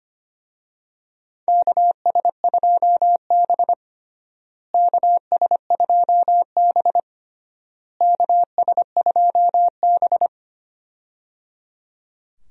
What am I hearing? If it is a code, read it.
Morse code, heard as a keyed tone, about 25 wpm, 700 Hz. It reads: KS2B KS2B KS2B
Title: USA Call Signs - 4 Characters - Repeated 3x - 25wpm